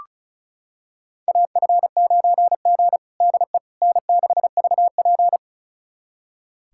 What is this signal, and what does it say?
Morse code, heard as a keyed tone, about 34 wpm, 700 Hz. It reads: AF9G DE N6VP
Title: USA Callsign DE Callsign - 4 Characters - 35wpm